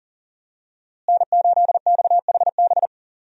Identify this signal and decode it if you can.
Morse code, heard as a keyed tone, about 40 wpm, 700 Hz. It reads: N8XHB
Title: USA Call Signs - 5 Characters - Repeated 2x - 40wpm